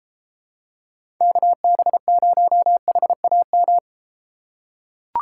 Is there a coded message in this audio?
Yes: Morse code from 1.2 to 3.8 s, heard as a keyed tone, about 34 wpm, 700 Hz. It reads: KB0HAM